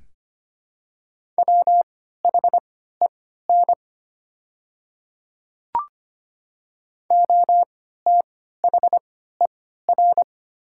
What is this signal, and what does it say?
Morse code, heard as a keyed tone, about 25 wpm, spaced as for 14 wpm, 700 Hz. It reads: WHEN OTHER
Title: 3x Character Spacing - Top 300 Words - 25wpm